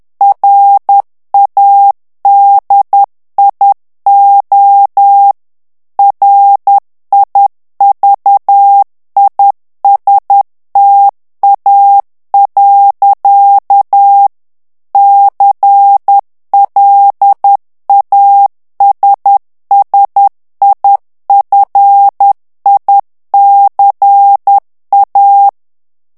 Morse code, heard as a keyed tone, about 10 wpm, 800 Hz. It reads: RADIO RIVISTA. CLASSIFICA
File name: ITALIEN texte clair 2 vitesse 10 mots minute